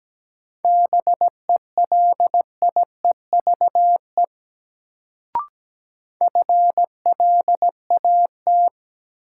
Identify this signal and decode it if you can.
Morse code, heard as a keyed tone, about 17 wpm, 700 Hz. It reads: BELIEVE FLAT
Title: Sentences from Top 1000 Words - New Word Drill - 17wpm